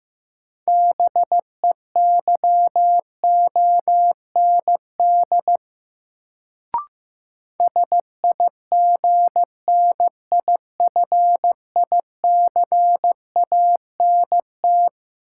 Morse code, heard as a keyed tone, about 15 wpm, 700 Hz. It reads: BEYOND SIGNIFICANT
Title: Sentences from Top 1000 Words - Review Part 6 - New Word Drill - 15wpm